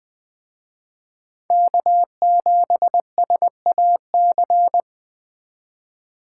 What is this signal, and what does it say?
Morse code, heard as a keyed tone, about 20 wpm, 700 Hz. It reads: K7SAC